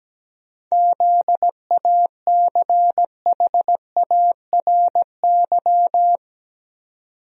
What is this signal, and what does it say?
Morse code, heard as a keyed tone, about 17 wpm, 700 Hz. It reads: ZACHARY
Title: USA Common Names - 17wpm